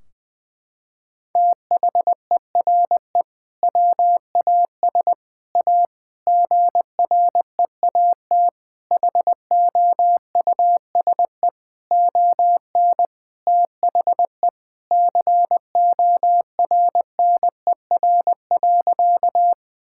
Morse code, heard as a keyed tone, about 20 wpm, 700 Hz. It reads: THERE WAS A GREAT HOUSE ON THE CORNER.
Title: Sentences from Top 1600 Words 20wpm